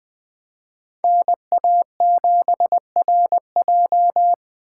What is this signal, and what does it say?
Morse code, heard as a keyed tone, about 21 wpm, 700 Hz. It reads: NA7RJ